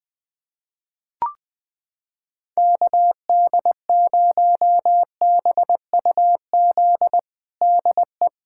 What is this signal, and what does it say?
Morse code, heard as a keyed tone, about 20 wpm, 700 Hz. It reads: KD0BUZ DE
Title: USA Callsign DE Callsign - 6 Characters - 20wpm